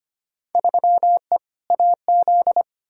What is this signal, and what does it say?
Morse code, heard as a keyed tone, about 25 wpm, 700 Hz. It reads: 3E AZ
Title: Field Day Exchanges Encore - 25wpm